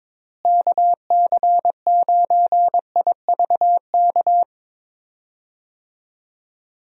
Morse code, heard as a keyed tone, about 22 wpm, 700 Hz. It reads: KC9IVK
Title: USA Call Signs - 6 Characters - Repeated 3x - 22wpm